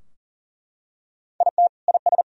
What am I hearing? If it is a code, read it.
Morse code, heard as a keyed tone, about 40 wpm, 700 Hz. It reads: IT IS
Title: Sentences from Top 1000 Words - Review Part 3 - 40wpm